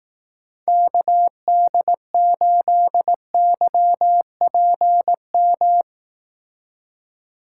Morse code, heard as a keyed tone, about 18 wpm, 700 Hz. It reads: KD8YPM
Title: Speed Racing - USA 2x3 Call Signs 15wpm